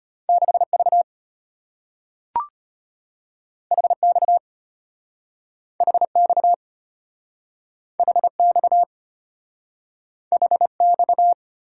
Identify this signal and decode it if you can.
Morse code, heard as a keyed tone, about 34 wpm, 700 Hz. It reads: 6V HX HX HX HX